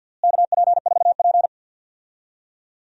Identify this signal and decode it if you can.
Morse code, heard as a keyed tone, about 46 wpm, 700 Hz. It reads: KP4P
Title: DXCC-100 Call Signs - 4 Characters - Repeated 3x - 50wpm